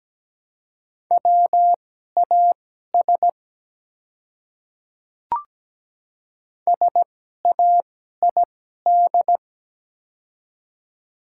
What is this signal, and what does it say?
Morse code, heard as a keyed tone, about 17 wpm, spaced as for 12 wpm, 700 Hz. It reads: WAS SAID
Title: 2x Character Spacing - Top 100 Words - 17wpm